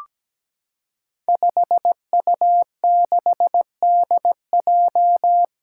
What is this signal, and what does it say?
Morse code, heard as a keyed tone, about 17 wpm, 700 Hz. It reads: 5U6DJ
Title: Five Challenging Letter-Number Combinations 17wpm